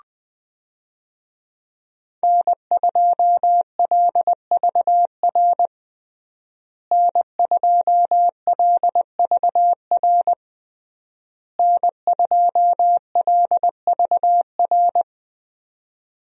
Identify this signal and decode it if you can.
Morse code, heard as a keyed tone, about 21 wpm, 700 Hz. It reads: N2LVR N2LVR N2LVR
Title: USA Call Signs - 5 Characters - Repeated 3x - 20wpm